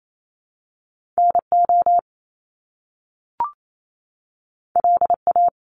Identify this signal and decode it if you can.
Morse code, heard as a keyed tone, about 28 wpm, 700 Hz. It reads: NO LA
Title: Top Two Letter Combinations in Words 28wpm